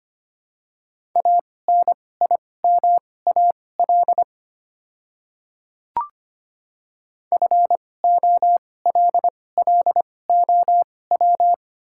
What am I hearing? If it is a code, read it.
Morse code, heard as a keyed tone, about 25 wpm, spaced as for 18 wpm, 700 Hz. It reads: ANIMAL FOLLOW
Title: 2x Character Spacing - Top 300 Words - 25wpm